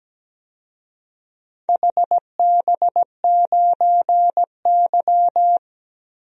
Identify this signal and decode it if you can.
Morse code, heard as a keyed tone, about 17 wpm, 700 Hz. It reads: HB9Y
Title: DXCC-100 Call Signs - 4 Characters - Repeated 3x - 17wpm